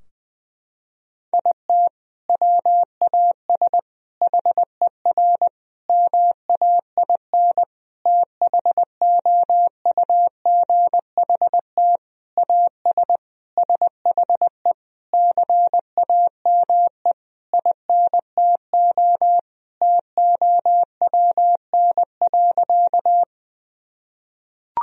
Morse code, heard as a keyed tone, about 20 wpm, 700 Hz. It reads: IT WAS HER MAIN THOUGHT AS SHE CAME INTO TOWN.
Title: Sentences from Top 1000 Words - Review Part 2 - 20wpm